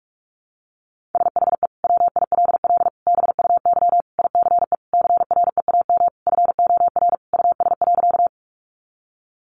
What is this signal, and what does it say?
Morse code, heard as a keyed tone, about 44 wpm, 700 Hz. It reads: SHE WILL BUY ICE CREAM FOR US.
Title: Sentences from Top 1500 Words 45wpm